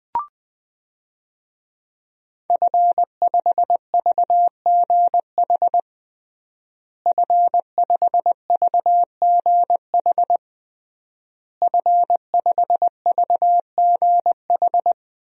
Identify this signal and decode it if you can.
Morse code, heard as a keyed tone, about 20 wpm, 700 Hz. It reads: F5VGH F5VGH F5VGH
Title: DXCC-100 Call Signs - 5 Characters - Repeated 3x - 20wpm